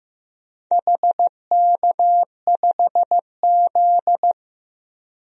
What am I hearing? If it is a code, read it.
Morse code, heard as a keyed tone, about 15 wpm, 700 Hz. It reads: HK5Z
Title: DXCC-100 Call Signs - 4 Characters - 15wpm